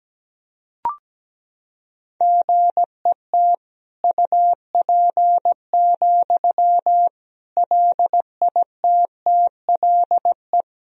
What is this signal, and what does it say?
Morse code, heard as a keyed tone, about 17 wpm, 700 Hz. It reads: GET UP, LITTLE
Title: Sentences from Top 1000 Words - Review Part 1 - 17wpm